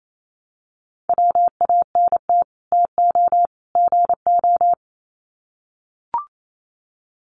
Morse code, heard as a keyed tone, about 28 wpm, 700 Hz. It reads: WANT TO GO
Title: Sets of 3 Words 28wpm